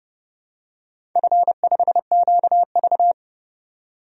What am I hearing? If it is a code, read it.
Morse code, heard as a keyed tone, about 30 wpm, 700 Hz. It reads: F5QV
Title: DXCC-100 Call Signs - 4 Characters - 30wpm